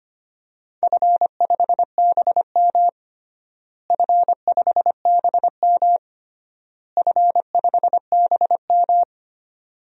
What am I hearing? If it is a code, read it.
Morse code, heard as a keyed tone, about 25 wpm, 700 Hz. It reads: F5BM F5BM F5BM
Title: DXCC-100 Call Signs - 4 Characters - Repeated 3x - 25wpm